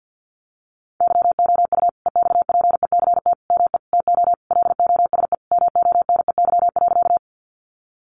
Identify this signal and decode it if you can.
Morse code, heard as a keyed tone, about 50 wpm, 700 Hz. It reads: YOU EXPECT ME TO LOSE MONEY.
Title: Sentences from Top 1100 Words 50wpm